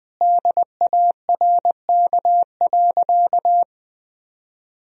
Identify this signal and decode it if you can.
Morse code, heard as a keyed tone, about 20 wpm, 700 Hz. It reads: DARK.
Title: Sentences from Top 500 Words and Abbreviations 20wpm